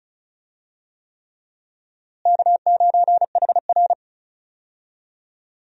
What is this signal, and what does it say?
Morse code, heard as a keyed tone, about 34 wpm, 700 Hz. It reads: K9HR
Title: USA Call Signs - 4 Characters - Repeated 3x - 35wpm